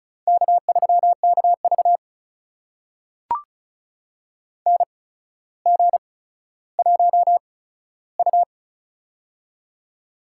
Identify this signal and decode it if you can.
Morse code, heard as a keyed tone, about 36 wpm, spaced as for 15 wpm, 700 Hz. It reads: K3KV NG1U
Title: USA Call Signs - 4 Characters - 8x Character Spacing - 35wpm